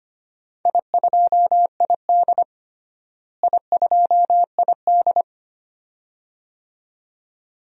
Morse code, heard as a keyed tone, about 25 wpm, 700 Hz. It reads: I2ID I2ID
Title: DXCC-100 Call Signs - 4 Characters - Repeated 2x - 25wpm